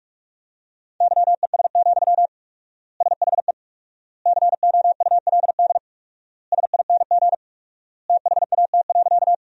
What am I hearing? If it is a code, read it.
Morse code, heard as a keyed tone, about 46 wpm, 700 Hz. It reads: YES, SHE COULD SING THAT.
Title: 4x Word-Spacing - Sentences from Top 500 Words - 45wpm